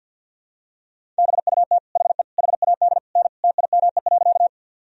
Morse code, heard as a keyed tone, about 50 wpm, 700 Hz. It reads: BUT HE HAD N TIME.
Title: Sentences from Top 100 Words and Abbreviations 50wpm